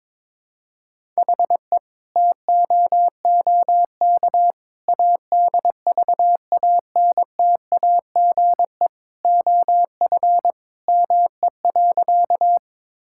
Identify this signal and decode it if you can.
Morse code, heard as a keyed tone, about 22 wpm, 700 Hz. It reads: HE TOOK ADVANTAGE OF ME.
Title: Sentences from Top 1000 Words - Review Part 6 - 22wpm